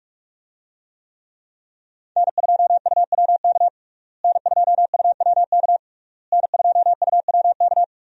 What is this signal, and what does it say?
Morse code, heard as a keyed tone, about 41 wpm, 700 Hz. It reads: N2UWK N2UWK N2UWK
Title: USA Call Signs - 5 Characters - Repeated 3x - 45wpm